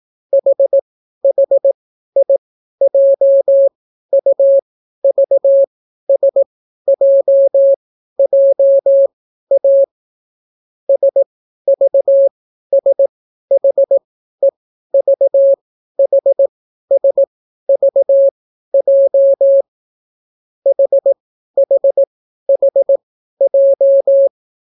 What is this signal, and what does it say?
Morse code, heard as a keyed tone, about 18 wpm, spaced as for 12 wpm, 550 Hz. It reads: HHIJUVSJJA SVSHEVHSVJ HHHJ